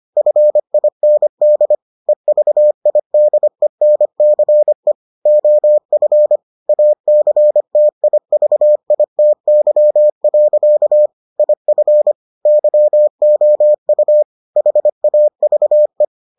Morse code, heard as a keyed tone, about 25 wpm, 600 Hz. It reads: FIND EVIDENCE OF ACTIVITY. IF YOU HAVE